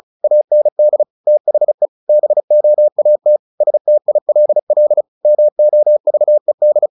Morse code, heard as a keyed tone, about 34 wpm, 600 Hz. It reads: AND THE BOAT STILL MOVED
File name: the old man and the sea / the old man and the sea 1 35wpm